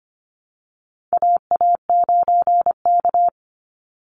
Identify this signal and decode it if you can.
Morse code, heard as a keyed tone, about 25 wpm, 700 Hz. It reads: AA9K
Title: Call Signs Repeated 3x - 25wpm